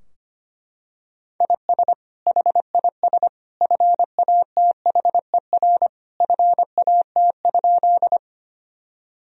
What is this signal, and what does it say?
Morse code, heard as a keyed tone, about 25 wpm, 700 Hz. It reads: IS HIS FATHER FAT?
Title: Sentences from Top 1000 Words - Review Part 4 - 25wpm